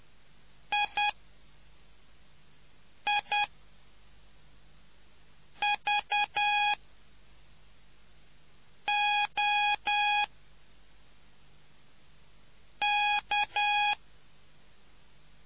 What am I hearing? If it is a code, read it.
Morse code, heard as a keyed tone, about 10 wpm, spaced as for 3 wpm, 830 Hz. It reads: IIVOK